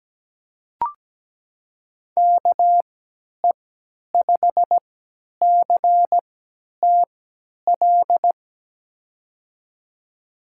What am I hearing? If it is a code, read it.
Morse code, heard as a keyed tone, about 17 wpm, spaced as for 10 wpm, 700 Hz. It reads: KE5CTL
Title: USA Call Signs - 6 Characters - 3x Character Spacing - 17wpm